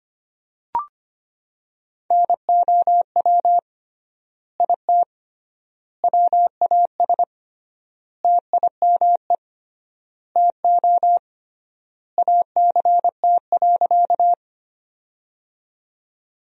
Morse code, heard as a keyed tone, about 25 wpm, 700 Hz. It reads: NOW IT WAS TIME TO ACT.
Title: 3x Word-Spacing - Sentences from Top 200 Words - 25wpm